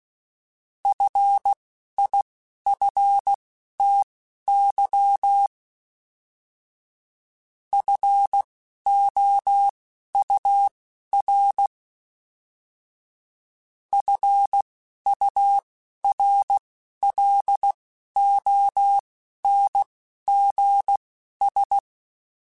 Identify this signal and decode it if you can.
Morse code, heard as a keyed tone, about 16 wpm, spaced as for 11 wpm, 800 Hz. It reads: FIFTY FOUR FURLONGS
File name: F Runs / F Runs mp3s / F Runs @10